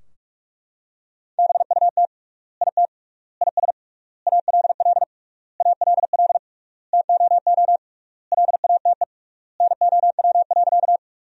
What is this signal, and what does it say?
Morse code, heard as a keyed tone, about 42 wpm, 700 Hz. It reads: BUT IT IS ALL ALL TOO LATE NOW.
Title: 3x Word-Spacing - Sentences from Top 300 Words - 45wpm